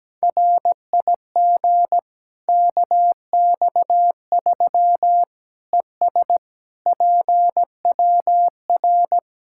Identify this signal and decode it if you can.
Morse code, heard as a keyed tone, about 17 wpm, 700 Hz. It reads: RIG KX3 ES PWR
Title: Practice QSO - Exchange 2 - Info - 17wpm